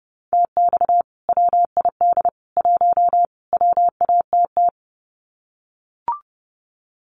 Morse code, heard as a keyed tone, about 30 wpm, 700 Hz. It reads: TX WID 1 WATT
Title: Common QSO Phrases 30wpm